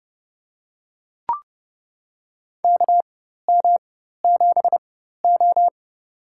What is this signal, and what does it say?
Morse code, heard as a keyed tone, about 30 wpm, spaced as for 14 wpm, 700 Hz. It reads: KM7O
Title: USA Call Signs - 4x Character Spacing - 30wpm